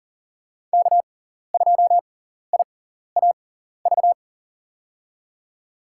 Morse code, heard as a keyed tone, about 37 wpm, spaced as for 14 wpm, 700 Hz. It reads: K2IAV
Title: USA Call Signs - 6x Character Spacing - 40wpm